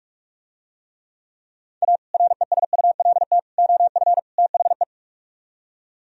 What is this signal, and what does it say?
Morse code, heard as a keyed tone, about 44 wpm, 700 Hz. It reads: A RESULT OF THE